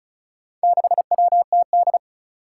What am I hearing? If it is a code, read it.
Morse code, heard as a keyed tone, about 36 wpm, 700 Hz. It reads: 6WTD
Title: Four Challenging Letter-Number Combinations 35wpm